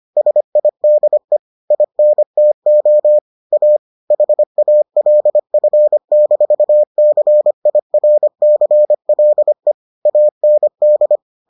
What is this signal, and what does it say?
Morse code, heard as a keyed tone, about 25 wpm, 600 Hz. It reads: SIDE INTO A HALF-CIRCLE AND